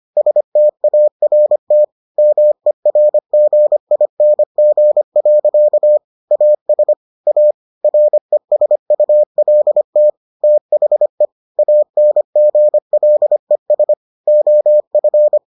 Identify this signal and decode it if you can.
Morse code, heard as a keyed tone, about 25 wpm, 600 Hz. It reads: START MERGING. AS A RESULT THE ANGLES OF